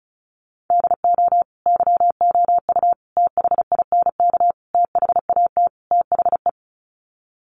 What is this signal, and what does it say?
Morse code, heard as a keyed tone, about 34 wpm, 700 Hz. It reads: DO YOU THINK THAT THE
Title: Sets of 5 Words Encore 35wpm